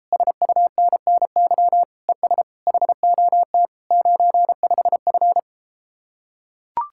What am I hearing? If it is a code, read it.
Morse code, heard as a keyed tone, about 33 wpm, 700 Hz. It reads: SUNNY ES HOT 95F